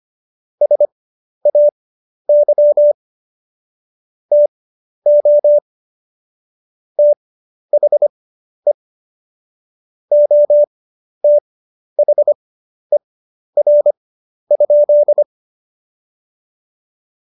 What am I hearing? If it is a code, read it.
Morse code, heard as a keyed tone, about 25 wpm, spaced as for 11 wpm, 590 Hz. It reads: SAY TO THE OTHER?
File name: Jokes-25-char-10-eff